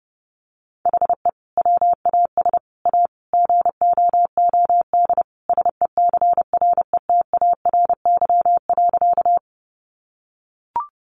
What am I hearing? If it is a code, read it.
Morse code, heard as a keyed tone, about 30 wpm, 700 Hz. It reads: HE WAS A GOOD SECRETARY.
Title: Sentences from Top 1300 Words 30wpm